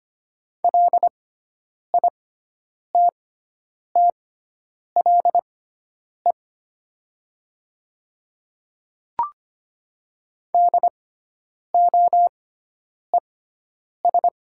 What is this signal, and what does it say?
Morse code, heard as a keyed tone, about 25 wpm, spaced as for 9 wpm, 700 Hz. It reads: LITTLE DOES
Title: 6x Character Spacing - Top 200 Words - 25wpm